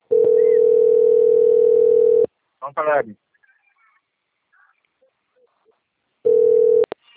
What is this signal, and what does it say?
Ringback tone (North America), 440 + 480 Hz, with speech.
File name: out- - - - - .33075